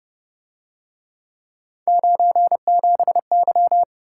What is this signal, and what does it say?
Morse code, heard as a keyed tone, about 31 wpm, 700 Hz. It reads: 97Y